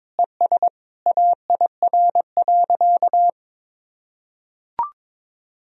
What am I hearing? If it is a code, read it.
Morse code, heard as a keyed tone, about 21 wpm, 700 Hz. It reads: ES AIR.
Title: Sentences from Top 400 Words and Abbreviations 22wpm